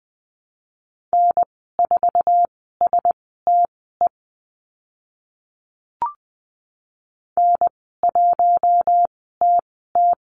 Morse code, heard as a keyed tone, about 20 wpm, spaced as for 14 wpm, 700 Hz. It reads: N4STE N1TT
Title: 055 Lesson 10g - Letter r - Call Signs 20wpm